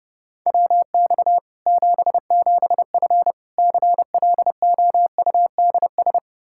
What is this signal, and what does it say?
Morse code, heard as a keyed tone, about 32 wpm, 700 Hz. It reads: WX 77F CLOUDS
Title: Practice QSO - Exchange 2 - Info - 30wpm